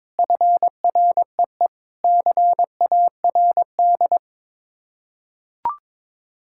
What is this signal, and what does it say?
Morse code, heard as a keyed tone, about 21 wpm, 700 Hz. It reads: FREE CARD
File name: practice / 22wpm / Sets of 7 Words - 22wpm